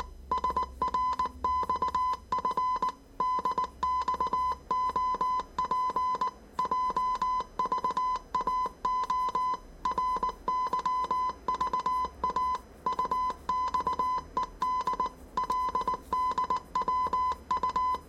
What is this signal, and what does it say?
Morse code, heard as a keyed tone, about 19 wpm, 1010 Hz. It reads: SRXF DXOP JVAO RYVA UXED LDWU
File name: 业余无线电 / ●未分类 / 字码80